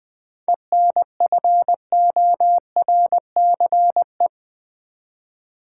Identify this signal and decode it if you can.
Morse code, heard as a keyed tone, about 20 wpm, 700 Hz. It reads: ENFORCE
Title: 7 Letter Words - First 3 Letters Given - 20wpm